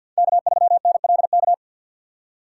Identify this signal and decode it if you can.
Morse code, heard as a keyed tone, about 52 wpm, 700 Hz. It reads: K3NLK
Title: USA Call Signs - 5 Characters - 4x Character Spacing - 50wpm